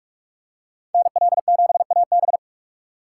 Morse code, heard as a keyed tone, about 46 wpm, 700 Hz. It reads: NL7AB
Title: USA Call Signs - 4x Character Spacing - 45wpm